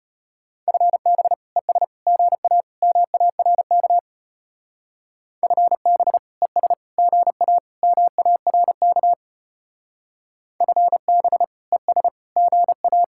Morse code, heard as a keyed tone, about 31 wpm, 700 Hz. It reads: FB ES GA MARK FB ES GA MARK FB ES GA
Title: Speed-Racing Common QSO Phrases Encore - 25wpm